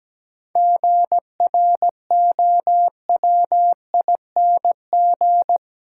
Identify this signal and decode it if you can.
Morse code, heard as a keyed tone, about 17 wpm, 700 Hz. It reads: GROWING
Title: Sets of 5 Words Encore - New Word Drill - 17wpm